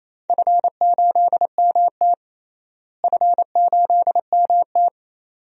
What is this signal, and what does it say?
Morse code, heard as a keyed tone, about 28 wpm, 700 Hz. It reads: F8MT F8MT
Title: DXCC-100 Call Signs - 4 Characters - Repeated 2x - 28wpm